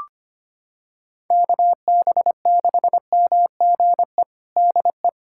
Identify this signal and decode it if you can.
Morse code, heard as a keyed tone, about 25 wpm, 700 Hz. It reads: KB6MGE DE
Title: USA Callsign DE Callsign 25wpm